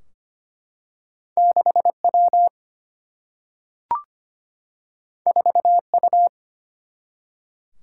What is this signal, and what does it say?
Morse code, heard as a keyed tone, about 25 wpm, 700 Hz. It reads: 6 W 4 U
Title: Single-Letter Single-Number 25wpm